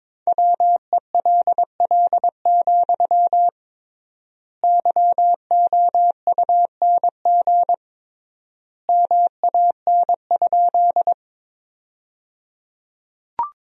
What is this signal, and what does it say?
Morse code, heard as a keyed tone, about 22 wpm, 700 Hz. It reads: WELL, YOUNG MAN?
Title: 3x Word-Spacing - Sentences from Top 400 Words - 22wpm